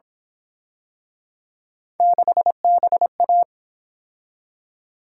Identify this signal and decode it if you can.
Morse code, heard as a keyed tone, about 29 wpm, 700 Hz. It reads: 6BA